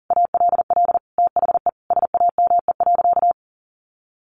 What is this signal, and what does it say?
Morse code, heard as a keyed tone, about 40 wpm, 700 Hz. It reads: ALL THE SAME.
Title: Sentences from Top 1500 Words 40wpm